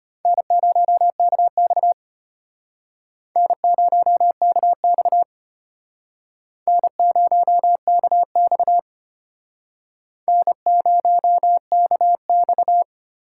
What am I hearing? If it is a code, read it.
Morse code, heard as a keyed tone, about 31 wpm, 700 Hz. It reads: N0KX N0KX N0KX N0KX
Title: Speed Racing - USA 1x2 Call Signs 25wpm